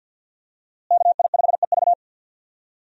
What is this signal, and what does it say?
Morse code, heard as a keyed tone, about 52 wpm, 700 Hz. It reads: KI5EV
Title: USA Call Signs - 5 Characters - Repeated 3x - 50wpm